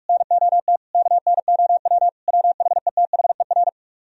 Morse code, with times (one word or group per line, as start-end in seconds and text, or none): 0.1-0.8 NOT
0.9-2.1 KNOW
2.3-3.7 WHETHER